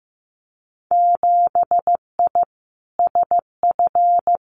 Morse code, heard as a keyed tone, about 15 wpm, 700 Hz. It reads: 7I SF
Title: Winter Field Day Exchanges Encore 15wpm